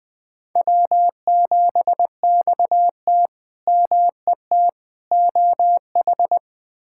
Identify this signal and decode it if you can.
Morse code, heard as a keyed tone, about 20 wpm, 700 Hz. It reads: W7XT MET OH
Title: K1USN SST QSO Practice - 20wpm